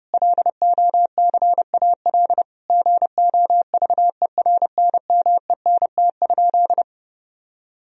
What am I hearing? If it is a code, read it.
Morse code, heard as a keyed tone, about 32 wpm, 700 Hz. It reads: LOCAL GOVERNMENT?
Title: Sentences from Top 1000 Words - Review Part 4 - 30wpm